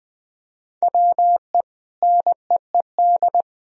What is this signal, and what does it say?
Morse code, heard as a keyed tone, about 21 wpm, 700 Hz. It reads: WE NEED
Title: Sentences from Top 1000 Words - Review Part 4 - 20wpm